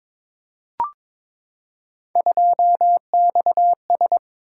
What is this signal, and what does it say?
Morse code, heard as a keyed tone, about 22 wpm, 700 Hz. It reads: 2XS
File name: practice / 22wpm / Three Letter-Number Combinations - 22wpm